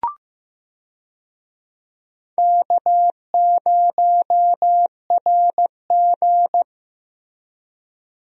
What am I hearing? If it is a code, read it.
Morse code, heard as a keyed tone, about 15 wpm, 700 Hz. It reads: K0RG
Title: USA Call Signs - 4 Characters - Repeated 2x - 15wpm